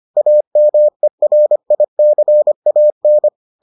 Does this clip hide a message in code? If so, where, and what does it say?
Yes: Morse code from 0.2 to 3.3 s, heard as a keyed tone, about 25 wpm, 600 Hz. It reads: AMERICAN